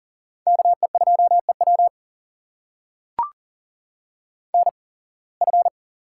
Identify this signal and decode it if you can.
Morse code, heard as a keyed tone, about 40 wpm, spaced as for 26 wpm, 700 Hz. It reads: KE2EW NF
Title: USA Call Signs - 8x Character Spacing - 40wpm